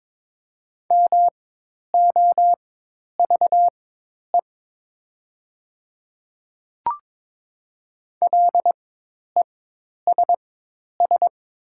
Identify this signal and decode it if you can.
Morse code, heard as a keyed tone, about 22 wpm, spaced as for 10 wpm, 700 Hz. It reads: MOVE LESS
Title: 4x Character Spacing - Top 500 Words - 22wpm